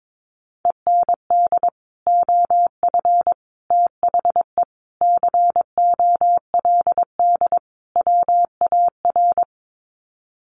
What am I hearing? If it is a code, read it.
Morse code, heard as a keyed tone, about 22 wpm, 700 Hz. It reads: END OF THE COLD WAR
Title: Sets of 5 Words Encore 22wpm